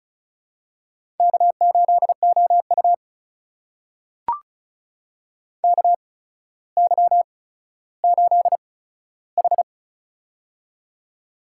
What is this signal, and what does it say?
Morse code, heard as a keyed tone, about 36 wpm, spaced as for 15 wpm, 700 Hz. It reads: K8OU KY8H
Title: USA Call Signs - 4 Characters - 8x Character Spacing - 35wpm